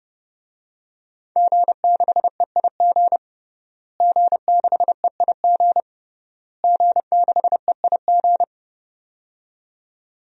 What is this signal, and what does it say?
Morse code, heard as a keyed tone, about 30 wpm, 700 Hz. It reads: G6EIG G6EIG G6EIG
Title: DXCC-100 Call Signs - 5 Characters - Repeated 3x - 30wpm